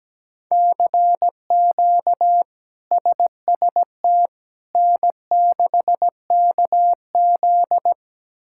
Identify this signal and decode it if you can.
Morse code, heard as a keyed tone, about 17 wpm, 700 Hz. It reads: CQ SST N6KZ
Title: K1USN SST QSO Practice - 17wpm